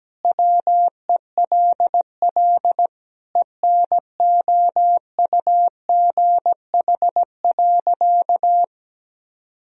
Morse code, heard as a keyed tone, about 17 wpm, 700 Hz. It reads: WELL ENOUGH.
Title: Sentences from Top 400 Words and Abbreviations 17wpm